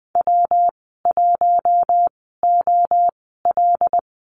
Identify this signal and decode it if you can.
Morse code, heard as a keyed tone, about 20 wpm, spaced as for 15 wpm, 700 Hz. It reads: W1OL
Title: 173 Lesson 28g - Number 9 - Call Signs 20wpm